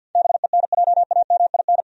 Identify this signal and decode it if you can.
Morse code, heard as a keyed tone, about 53 wpm, 700 Hz. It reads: BENJAMIN
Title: USA Common Names - 50wpm